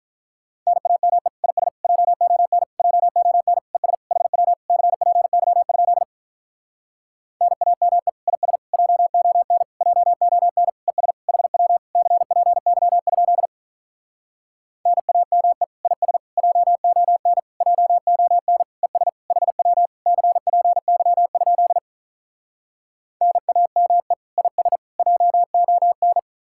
Morse code, heard as a keyed tone, about 48 wpm, 700 Hz. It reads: NAME IS JON JON ES HW CPY? NAME IS JON JON ES HW CPY? NAME IS JON JON ES HW CPY? NAME IS JON